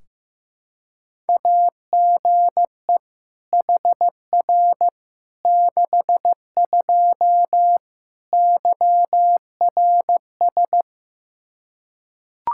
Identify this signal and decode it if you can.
Morse code, heard as a keyed tone, about 15 wpm, 700 Hz. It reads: AGE HR 62 YRS
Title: Practice QSO - Exchange 3 - More Info - 15wpm